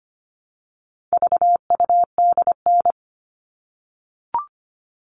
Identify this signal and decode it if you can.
Morse code, heard as a keyed tone, about 25 wpm, 700 Hz. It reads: VUDN
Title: All Cut Numbers - 4-Digit 25wpm